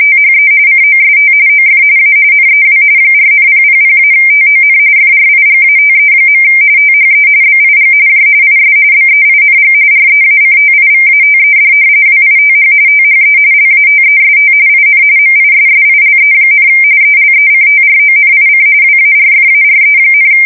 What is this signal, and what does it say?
RTTY, about 45.45 baud. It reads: THE COST OF A NEW 4,000-TON FRIGATE. THE BIG J IS BEING REACTIVATED WITH THE LATEST IN MISSILE, ELECTRONIC WARFARE,
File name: fsk170 / battleshipnj